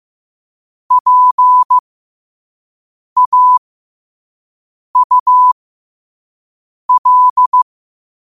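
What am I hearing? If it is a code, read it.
Morse code, heard as a keyed tone, about 15 wpm, spaced as for 5 wpm, 1000 Hz. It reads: PAUL
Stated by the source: Morse code quote of the day at 5 wpm.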